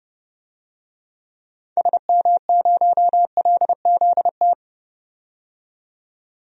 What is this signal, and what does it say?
Morse code, heard as a keyed tone, about 30 wpm, 700 Hz. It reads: SM0LZT
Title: DXCC-100 Call Signs - 6 Characters - 30wpm